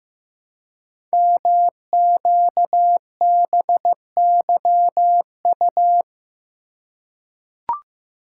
Morse code, heard as a keyed tone, about 15 wpm, 700 Hz. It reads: MQBYU
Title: Code Groups - 5 Letters - 15wpm